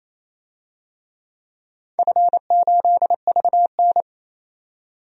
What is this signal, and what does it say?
Morse code, heard as a keyed tone, about 28 wpm, 700 Hz. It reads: F8VN
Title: DXCC-100 Call Signs - 4 Characters - 28wpm